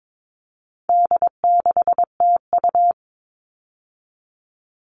Morse code, heard as a keyed tone, about 22 wpm, 700 Hz. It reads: D6TU
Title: All Cut Numbers - 4-Digit 22wpm